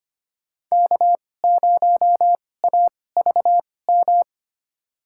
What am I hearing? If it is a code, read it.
Morse code, heard as a keyed tone, about 25 wpm, spaced as for 18 wpm, 700 Hz. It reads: K0AVM
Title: USA Call Signs - 5 Characters - 2x Character Spacing - 25wpm